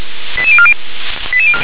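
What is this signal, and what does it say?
Five-tone selective call (ZVEI), heard as quick runs of five tones, tones of about 70 ms. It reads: 88040
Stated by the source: ZVEI five-tone call 88177 received by SDR.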